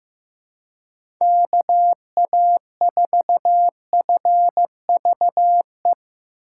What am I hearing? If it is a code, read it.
Morse code, heard as a keyed tone, about 15 wpm, 700 Hz. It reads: KA4FVE
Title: USA Call Signs - 6 Characters - 8x Character Spacing - 15wpm